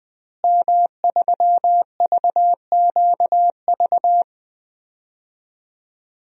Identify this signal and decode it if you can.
Morse code, heard as a keyed tone, about 20 wpm, 700 Hz. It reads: M3VQV
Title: DXCC-100 Call Signs - 5 Characters - Repeated 3x - 20wpm